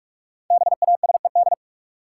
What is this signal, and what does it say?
Morse code, heard as a keyed tone, about 47 wpm, 700 Hz. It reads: BASED
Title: Sets of 2 Words - New Word Drill - 45wpm